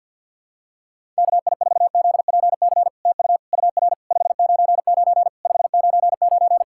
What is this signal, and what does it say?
Morse code, heard as a keyed tone, about 50 wpm, 660 Hz. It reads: KI4ZPC TU UR 599 599
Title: SOTA QSO Practice 50wpm